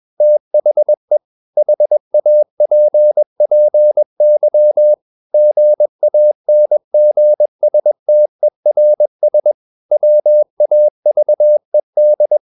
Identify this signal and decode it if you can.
Morse code, heard as a keyed tone, about 21 wpm, 600 Hz. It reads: THE HAPPY GANGSTERS WAVED
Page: Romanen Never af Ken Follet på engelsk.